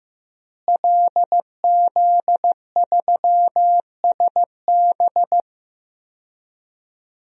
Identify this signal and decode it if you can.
Morse code, heard as a keyed tone, about 15 wpm, 700 Hz. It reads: LZ3SB
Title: DXCC-100 Call Signs - 5 Characters - Repeated 3x - 15wpm